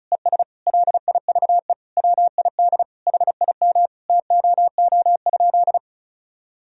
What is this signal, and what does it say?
Morse code, heard as a keyed tone, about 36 wpm, 700 Hz. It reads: ES LIVE WID HIM TOO?
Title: Sentences from Top 500 Words and Abbreviations 35wpm